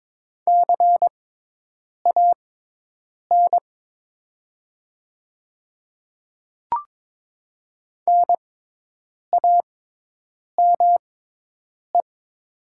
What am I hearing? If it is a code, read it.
Morse code, heard as a keyed tone, about 22 wpm, spaced as for 8 wpm, 700 Hz. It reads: CAN NAME
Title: 6x Character Spacing - Top 300 Words - 22wpm